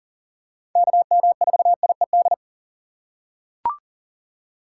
Morse code, heard as a keyed tone, about 40 wpm, 700 Hz. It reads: KM4IED
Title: USA Call Signs - 6 Characters - 2x Character Spacing - 40wpm